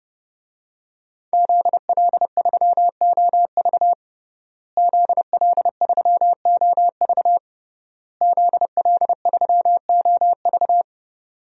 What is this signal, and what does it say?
Morse code, heard as a keyed tone, about 30 wpm, 700 Hz. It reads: ZL3OV ZL3OV ZL3OV
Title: DXCC-100 Call Signs - 5 Characters - Repeated 3x - 30wpm